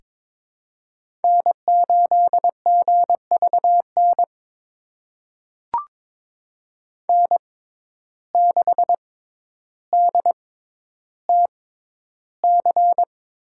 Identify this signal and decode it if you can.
Morse code, heard as a keyed tone, about 22 wpm, spaced as for 11 wpm, 700 Hz. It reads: N8GVN N6DTC
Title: USA Call Signs - 5 Characters - 6x Character Spacing - 22wpm